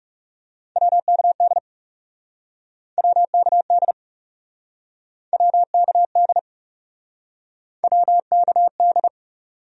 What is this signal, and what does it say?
Morse code, heard as a keyed tone, about 43 wpm, 700 Hz. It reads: WKD WKD WKD WKD